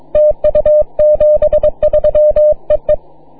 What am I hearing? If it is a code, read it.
Morse code, heard as a keyed tone, about 22 wpm, 600 Hz. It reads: TU 73 EE
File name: wp / JG3LEB heard by LA8AJA on 6m